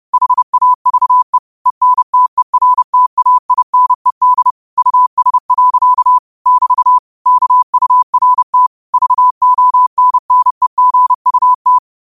Morse code, heard as a keyed tone, about 30 wpm, 1000 Hz. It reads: HAVE ENTERTAINED US. <BT> KURT VONNEGUT
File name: qotdfiles / 30wpm